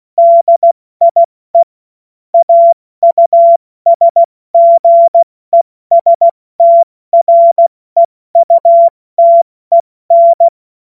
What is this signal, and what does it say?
Morse code, heard as a keyed tone, about 16 wpm, spaced as for 14 wpm, 680 Hz. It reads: DIE AUSGESTREUTEN
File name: Witz-15-Juli-2025-70-F80